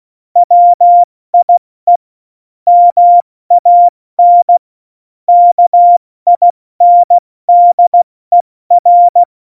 Morse code, heard as a keyed tone, about 16 wpm, 700 Hz. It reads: WIE MAN KINDER
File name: Morsekurs / Witze / Witz 25. Juni 2019, 70 F80